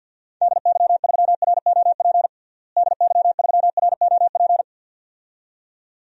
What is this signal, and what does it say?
Morse code, heard as a keyed tone, about 53 wpm, 700 Hz. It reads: DY3ROP DY3ROP
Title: DXCC-100 Call Signs - 6 Characters - Repeated 2x - 50wpm